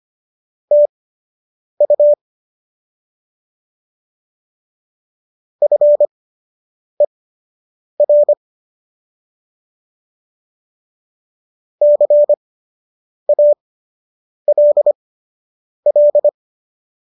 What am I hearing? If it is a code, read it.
Morse code, heard as a keyed tone, about 25 wpm, spaced as for 8 wpm, 590 Hz. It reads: TU FER CALL